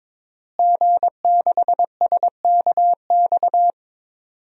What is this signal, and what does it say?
Morse code, heard as a keyed tone, about 22 wpm, 700 Hz. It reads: G6SKX
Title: DXCC-100 Call Signs - 5 Characters - Repeated 2x - 22wpm